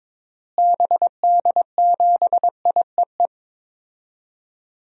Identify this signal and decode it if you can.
Morse code, heard as a keyed tone, about 22 wpm, 700 Hz. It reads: BD7IEE
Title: DXCC-100 Call Signs - 6 Characters - Repeated 2x - 22wpm